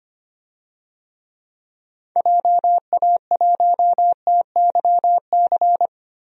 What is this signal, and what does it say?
Morse code, heard as a keyed tone, about 25 wpm, 700 Hz. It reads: JA1TYC
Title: DXCC-100 Call Signs - 6 Characters - Repeated 2x - 25wpm